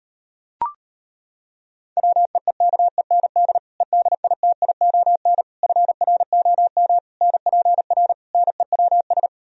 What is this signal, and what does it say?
Morse code, heard as a keyed tone, about 35 wpm, 700 Hz. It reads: WEEKEND EDITION FROM NPR NEWS